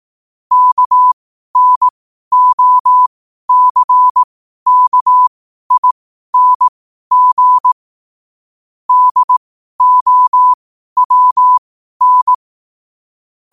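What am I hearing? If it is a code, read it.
Morse code, heard as a keyed tone, about 18 wpm, spaced as for 12 wpm, 1000 Hz. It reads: KNOCKING DOWN
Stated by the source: 12 wpm Morse.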